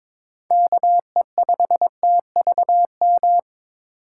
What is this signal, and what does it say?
Morse code, heard as a keyed tone, about 22 wpm, 700 Hz. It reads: KE5TVM
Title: USA Call Signs - 6 Characters - 2x Character Spacing - 22wpm